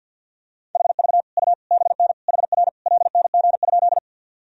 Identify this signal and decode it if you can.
Morse code, heard as a keyed tone, about 46 wpm, 700 Hz. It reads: HV U BN HR LNG?
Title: Sentences from Top 200 Words and Abbreviations 50wpm